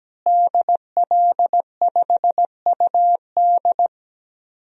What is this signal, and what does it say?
Morse code, heard as a keyed tone, about 17 wpm, 700 Hz. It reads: DL5UD
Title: DXCC-100 Call Signs - 5 Characters - Repeated 2x - 17wpm